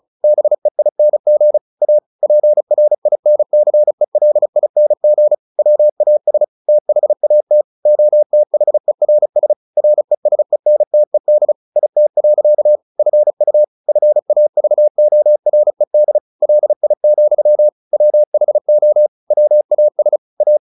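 Morse code, heard as a keyed tone, about 34 wpm, 600 Hz. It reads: BEING A PRINCELING WAS THAT OTHERS RESENTED IT. FU FAVORED LI, WHO WAS A
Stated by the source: Morse practice at 35 wpm from CW med Gnister, sending Ken Follett's novel Never in English.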